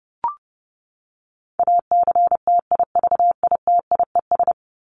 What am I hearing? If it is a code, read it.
Morse code, heard as a keyed tone, about 30 wpm, 700 Hz. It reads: ACTIVITIES
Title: ES:S Suffix - Top 100 Words with Base Words 30wpm